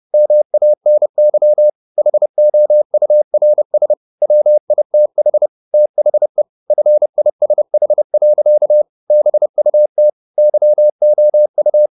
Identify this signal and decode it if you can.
Morse code, heard as a keyed tone, about 30 wpm, 600 Hz. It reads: MANY HOURS WITH THE FISH. BUT YOU
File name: the old man and the sea / the old man and the sea 1 30wpm